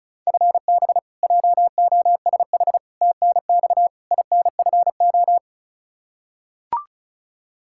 Morse code, heard as a keyed tone, about 36 wpm, 700 Hz. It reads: FB JOSH TNX INFO
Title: Practice QSO - Exchange 3 - More Info - 35wpm